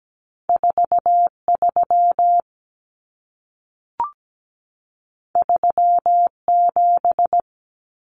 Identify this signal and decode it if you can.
Morse code, heard as a keyed tone, about 17 wpm, 700 Hz. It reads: 43 37
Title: Common Cut Numbers - 2-Digit 17wpm